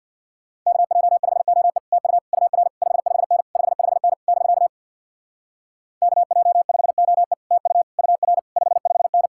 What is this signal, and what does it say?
Morse code, heard as a keyed tone, about 62 wpm, 700 Hz. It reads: KJ5OE TU UR 55N 55N <BK> KJ5OE TU UR 55N
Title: Speed-Racing Common QSO Phrases - 50wpm